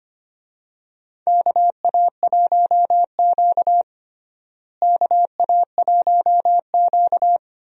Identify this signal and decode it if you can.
Morse code, heard as a keyed tone, about 25 wpm, 700 Hz. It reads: KA1Q KA1Q
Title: USA Call Signs - 4 Characters - Repeated 2x - 25wpm